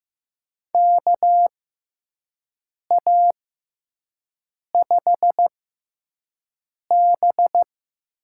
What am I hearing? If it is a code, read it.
Morse code, heard as a keyed tone, about 15 wpm, spaced as for 5 wpm, 700 Hz. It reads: KA5B
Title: USA Call Signs - 6 Characters - 6x Character Spacing - 15wpm